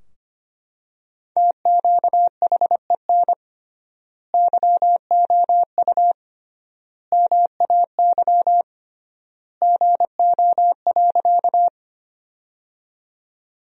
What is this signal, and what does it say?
Morse code, heard as a keyed tone, about 25 wpm, 700 Hz. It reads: TQHEN YOU MAY GO.
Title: 3x Word-Spacing - Sentences from Top 100 Words - 25wpm